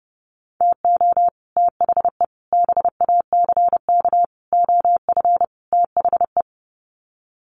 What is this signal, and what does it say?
Morse code, heard as a keyed tone, about 30 wpm, 700 Hz. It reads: TO THE BACK OF THE
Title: Sets of 5 Words 30wpm